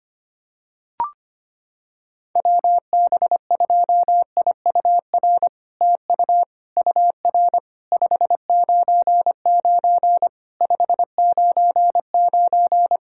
Morse code, heard as a keyed tone, about 25 wpm, 700 Hz. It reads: WB2IUR TU UR 599 599
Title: SOTA QSO Practice 25wpm